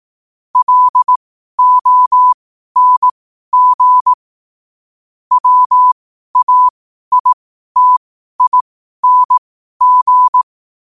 Morse code, heard as a keyed tone, about 18 wpm, spaced as for 12 wpm, 1000 Hz. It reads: LONG WAITING
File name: qotdfiles / 12wpm